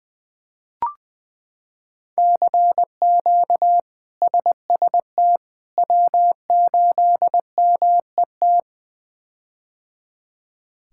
Morse code, heard as a keyed tone, about 20 wpm, 700 Hz. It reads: CQ SST W8MET
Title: K1USN SST QSO Practice - 20wpm